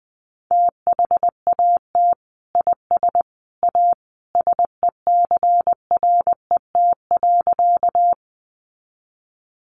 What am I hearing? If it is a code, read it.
Morse code, heard as a keyed tone, about 20 wpm, 700 Hz. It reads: THAT IS A SECRET.
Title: Sentences from Top 1200 Words 20wpm